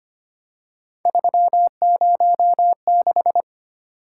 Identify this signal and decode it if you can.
Morse code, heard as a keyed tone, about 25 wpm, 700 Hz. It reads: 306
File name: Three Letter-Number Combinations - 25wpm